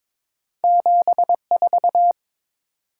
Morse code, heard as a keyed tone, about 22 wpm, 700 Hz. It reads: 74
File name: practice / 22wpm / Two Letter-Number Combinations - 22wpm